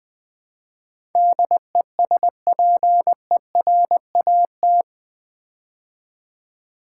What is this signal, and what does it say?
Morse code, heard as a keyed tone, about 20 wpm, 700 Hz. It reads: DESPERAT
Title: Build A Word - 4 Letters At A Time - 11-Letter Words - 20wpm